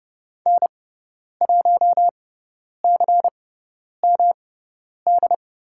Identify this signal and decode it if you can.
Morse code, heard as a keyed tone, about 30 wpm, spaced as for 10 wpm, 700 Hz. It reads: N1CMD
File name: Lesson 18g - Letter m - Call Signs 30wpm